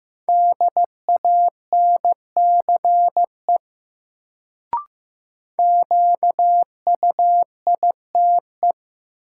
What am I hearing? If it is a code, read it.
Morse code, heard as a keyed tone, about 15 wpm, 700 Hz. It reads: DANCE QUITE
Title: Sentences from Top 700 Words - New Word Drill - 15wpm